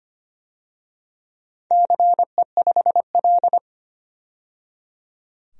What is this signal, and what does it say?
Morse code, heard as a keyed tone, about 25 wpm, 700 Hz. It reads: CE5L
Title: DXCC-100 Call Signs - 4 Characters - 25wpm